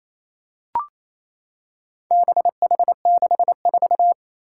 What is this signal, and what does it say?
Morse code, heard as a keyed tone, about 29 wpm, 700 Hz. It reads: BH64
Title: Four Challenging Letter-Number Combinations 28wpm